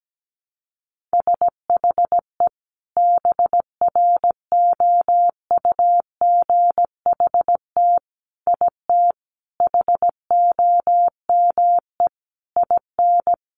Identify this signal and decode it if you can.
Morse code, heard as a keyed tone, about 17 wpm, 700 Hz. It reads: SHE BROUGHT IT HOME IN
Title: Sentences from Top 500 Words 17wpm